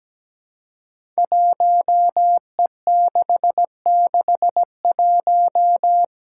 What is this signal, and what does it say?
Morse code, heard as a keyed tone, about 17 wpm, 700 Hz. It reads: 1E661
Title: Five Challenging Letter-Number Combinations 17wpm